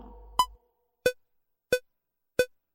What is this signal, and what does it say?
There is no code.